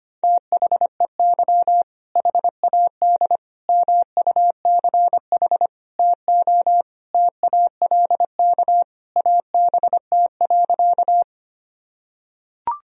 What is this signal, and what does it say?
Morse code, heard as a keyed tone, about 25 wpm, 700 Hz. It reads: THEY HAD MUCH TO TALK ABT.
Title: Sentences from Top 400 Words and Abbreviations 25wpm